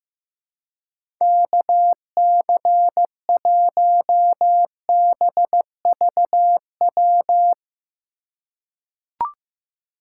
Morse code, heard as a keyed tone, about 15 wpm, 700 Hz. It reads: KC1BVW
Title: USA Call Signs - 6 Characters - 4x Character Spacing - 15wpm